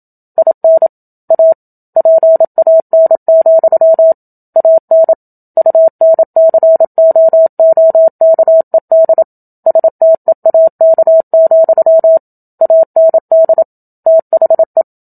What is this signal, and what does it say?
Morse code, heard as a keyed tone, about 27 wpm, 650 Hz. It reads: IN A PAN, AN UNCOOKED STEAK, AND THE
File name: Book Two / 01-Chapter 1 - 27 WPM